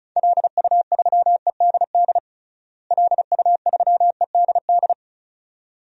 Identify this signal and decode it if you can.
Morse code, heard as a keyed tone, about 34 wpm, 700 Hz. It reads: LU3EDD LU3EDD
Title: DXCC-100 Call Signs - 6 Characters - Repeated 2x - 35wpm